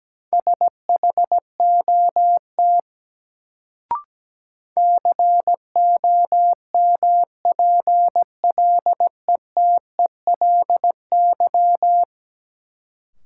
Morse code, heard as a keyed tone, about 17 wpm, 700 Hz. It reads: SHOT COMPLETELY
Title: Sentences from Top 1100 Words - New Word Drill Alt - 17wpm